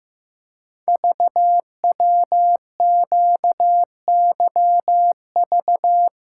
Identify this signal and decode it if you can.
Morse code, heard as a keyed tone, about 15 wpm, 700 Hz. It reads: VWQYV
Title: Code Groups - 5 Letters - 15wpm